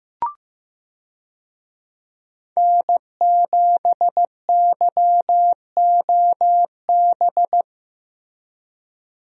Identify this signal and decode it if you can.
Morse code, heard as a keyed tone, about 14 wpm, 700 Hz. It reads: N7YOB
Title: USA Call Signs - 5 Characters - Repeated 2x - 15wpm